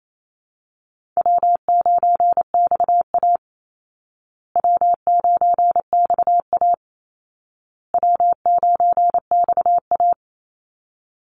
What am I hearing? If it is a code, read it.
Morse code, heard as a keyed tone, about 28 wpm, 700 Hz. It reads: W9XA W9XA W9XA
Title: Call Signs Repeated 2x - 28wpm